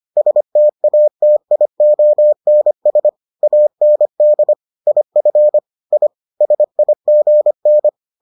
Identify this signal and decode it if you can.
Morse code, heard as a keyed tone, about 25 wpm, 600 Hz. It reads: STATIONS AND IF I SIGN